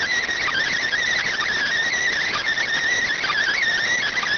Emergency Alert System (SAME) tones, heard as harsh buzzing bursts. It is heard from 0.8 to 3.6 s.